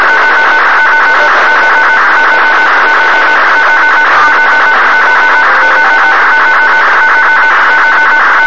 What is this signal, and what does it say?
Two-tone teleprinter signal, about 200 Hz shift.